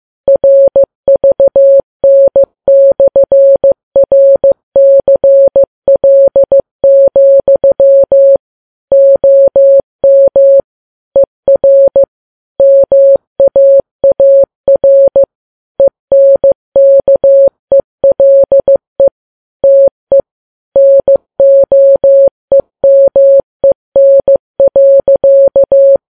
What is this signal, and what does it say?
Morse code, heard as a keyed tone, about 15 wpm, 550 Hz. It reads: RVN/RCL, OM ER MAAR ENKELE TE NOEMEN.
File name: Files Web tls / Morse on9fv / morse15kps